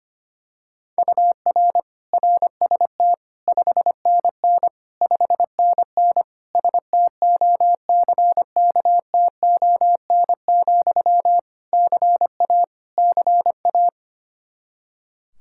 Morse code, heard as a keyed tone, about 25 wpm, 700 Hz. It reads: UR RST 5NN 5NN STOCKTON, CA CA
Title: Practice QSO - Exchange 1 - Beginning - 25wpm